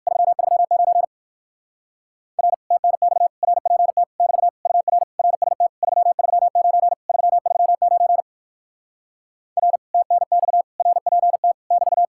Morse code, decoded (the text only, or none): 339 R TNX RPT <BT> UR RST 339 339 R TNX RPT <BT>